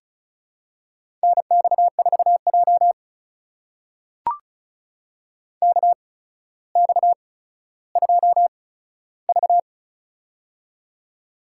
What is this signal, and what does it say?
Morse code, heard as a keyed tone, about 36 wpm, spaced as for 15 wpm, 700 Hz. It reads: NX4J KX2V
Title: USA Call Signs - 4 Characters - 8x Character Spacing - 35wpm